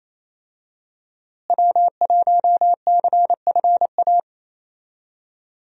Morse code, heard as a keyed tone, about 27 wpm, 700 Hz. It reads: W1CFA
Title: USA Call Signs - 5 Characters - Repeated 3x - 28wpm